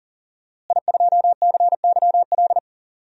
Morse code, heard as a keyed tone, about 40 wpm, 700 Hz. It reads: I2CYL
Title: DXCC-100 Call Signs - 5 Characters - Repeated 3x - 40wpm